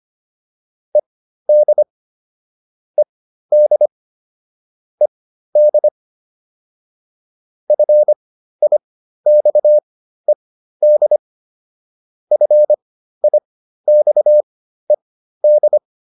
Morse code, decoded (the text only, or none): ED ED ED FIXED FIXED